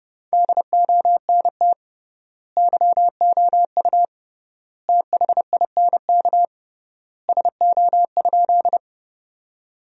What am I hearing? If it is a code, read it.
Morse code, heard as a keyed tone, about 30 wpm, 700 Hz. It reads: DONT YOU THINK SO?
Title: 3x Word-Spacing - Sentences from Top 300 Words - 30wpm